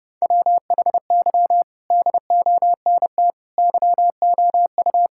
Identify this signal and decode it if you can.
Morse code, heard as a keyed tone, about 30 wpm, 700 Hz. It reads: WHY DONT YOU
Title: Sentences from Top 1000 Words - Review Part 5 - 30wpm